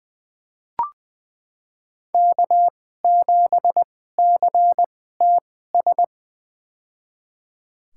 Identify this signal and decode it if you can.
Morse code, heard as a keyed tone, about 20 wpm, spaced as for 14 wpm, 700 Hz. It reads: K7CTS
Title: USA Call Signs - 5 Characters - 2x Character Spacing - 20wpm